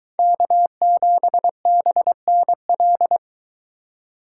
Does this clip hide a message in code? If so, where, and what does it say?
Yes: Morse code from 0.2 to 3.2 s, heard as a keyed tone, about 23 wpm, 700 Hz. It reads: K7BNL